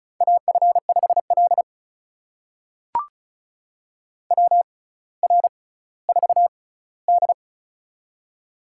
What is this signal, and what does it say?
Morse code, heard as a keyed tone, about 36 wpm, spaced as for 18 wpm, 700 Hz. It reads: AF5L WR4D
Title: USA Call Signs - 6x Character Spacing - 35wpm